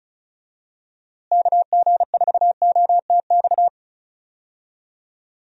Morse code, heard as a keyed tone, about 34 wpm, 700 Hz. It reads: KG4OTX
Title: USA Call Signs - 6 Characters - Repeated 3x - 35wpm